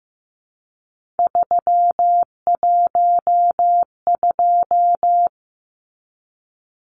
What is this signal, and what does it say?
Morse code, heard as a keyed tone, about 15 wpm, 700 Hz. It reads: 312
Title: Numbers - 3-Digit 15wpm